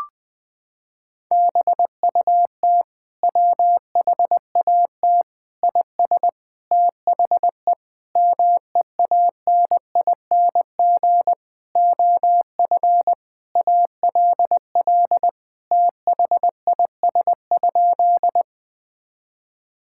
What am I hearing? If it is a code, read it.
Morse code, heard as a keyed tone, about 20 wpm, 700 Hz. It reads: BUT WHAT IS THE MEANING OF ALL THIS?
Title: Sentences from Top 1000 Words - Review Part 6 - 20wpm